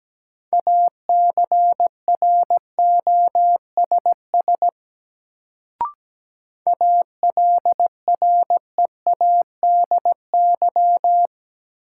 Morse code, heard as a keyed tone, about 17 wpm, 700 Hz. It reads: ACROSS ALREADY
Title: Sentences from Top 700 Words - New Word Drill - 17wpm